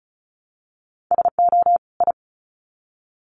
Morse code, heard as a keyed tone, about 34 wpm, 700 Hz. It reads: SO I
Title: Sets of 2 Words 35wpm